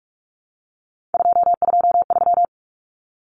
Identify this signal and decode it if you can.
Morse code, heard as a keyed tone, about 44 wpm, 700 Hz. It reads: 223
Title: Common Cut Numbers - 3-Digit 45wpm